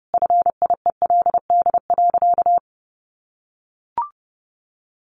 Morse code, heard as a keyed tone, about 30 wpm, 700 Hz. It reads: FIELD.
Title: Easy Phrases 30wpm